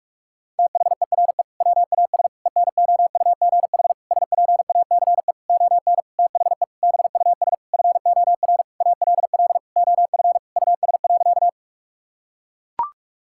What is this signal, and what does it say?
Morse code, heard as a keyed tone, about 41 wpm, 700 Hz. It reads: THERE WAS ENOUGH SPACE ON THE BUS FOR ALL OF US.
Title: Sentences from Top 1000 Words - Review Part 6 - 45wpm